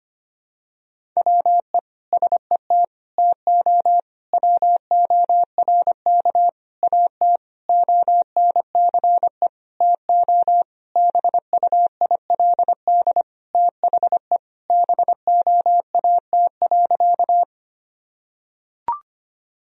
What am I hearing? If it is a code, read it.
Morse code, heard as a keyed tone, about 25 wpm, 700 Hz. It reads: WE SET TO WORK AT ONCE TO BUILD THE BOAT.
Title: Sentences from Top 1000 Words - Review Part 1 - 25wpm